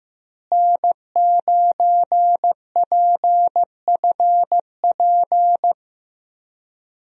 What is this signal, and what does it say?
Morse code, heard as a keyed tone, about 15 wpm, 700 Hz. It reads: N9PFP
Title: USA Call Signs - 5 Characters - Repeated 2x - 15wpm